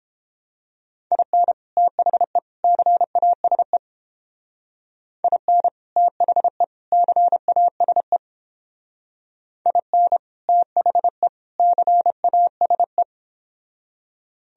Morse code, heard as a keyed tone, about 29 wpm, 700 Hz. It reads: IN THE CASE IN THE CASE IN THE CASE